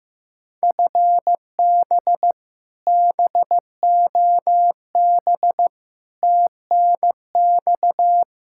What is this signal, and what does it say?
Morse code, heard as a keyed tone, about 15 wpm, 700 Hz. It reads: FB BOB TNX
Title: Practice QSO - Exchange 3 - More Info - 15wpm